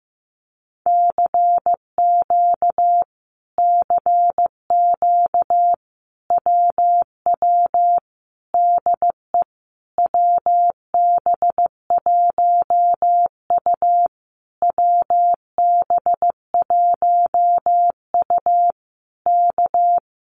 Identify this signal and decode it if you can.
Morse code, heard as a keyed tone, about 15 wpm, 700 Hz. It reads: CQ CQ WW DE WB1U WB1U K
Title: Common QSO Phrases Encore 15wpm